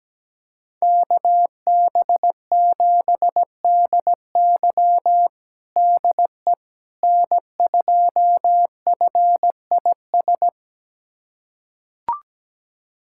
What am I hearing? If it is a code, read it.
Morse code, heard as a keyed tone, about 17 wpm, 700 Hz. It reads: KB7DY DE N2FIS
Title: USA Callsign DE Callsign - 5 Characters - 17wpm